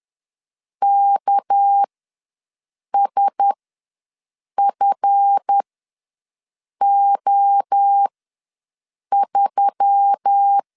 Morse code, heard as a keyed tone, about 10 wpm, spaced as for 6 wpm, 800 Hz. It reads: KSFO3
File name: LEZIONI / 7 GF9Z / 7 GF ws 10wpm 3918